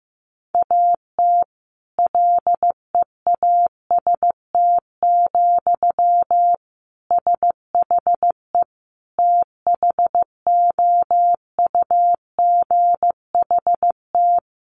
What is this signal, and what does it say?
Morse code, heard as a keyed tone, about 15 wpm, 700 Hz. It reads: AT LEAST, SHE THOUGHT
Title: Sentences from Top 1400 Words 15wpm